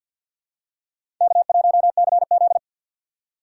Morse code, heard as a keyed tone, about 53 wpm, 700 Hz. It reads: K1CZ
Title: USA Call Signs - 4x Character Spacing - 50wpm